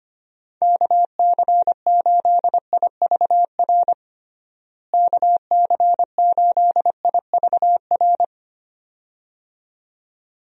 Morse code, heard as a keyed tone, about 25 wpm, 700 Hz. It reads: KC8IVR KC8IVR
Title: USA Call Signs - 6 Characters - Repeated 2x - 25wpm